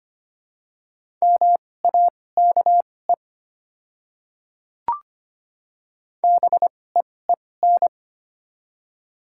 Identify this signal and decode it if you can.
Morse code, heard as a keyed tone, about 25 wpm, spaced as for 18 wpm, 690 Hz. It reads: MAKE BEEN
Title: 2x Character Spacing - Top 200 Words - 25wpm